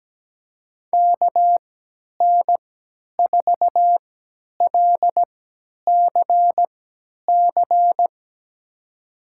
Morse code, heard as a keyed tone, about 17 wpm, spaced as for 10 wpm, 700 Hz. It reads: KN4LCC
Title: USA Call Signs - 3x Character Spacing - 17wpm